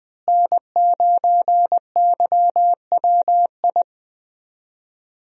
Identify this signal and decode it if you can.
Morse code, heard as a keyed tone, about 20 wpm, 700 Hz. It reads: N9YWI
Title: USA Call Signs - 5 Characters - Repeated 3x - 20wpm